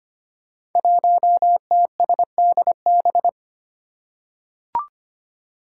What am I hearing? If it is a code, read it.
Morse code, heard as a keyed tone, about 25 wpm, 700 Hz. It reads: 1TSDB
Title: Five Challenging Letter-Number Combinations 25wpm